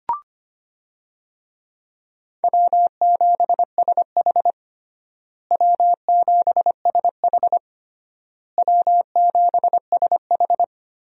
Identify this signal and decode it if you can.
Morse code, heard as a keyed tone, about 25 wpm, 700 Hz. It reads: W7SH W7SH W7SH
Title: USA Call Signs - 4 Characters - Repeated 3x - 25wpm